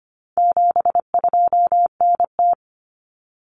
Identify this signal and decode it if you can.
Morse code, heard as a keyed tone, about 25 wpm, 700 Hz. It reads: 72NT
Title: Common Cut Numbers - 4-Digit 25wpm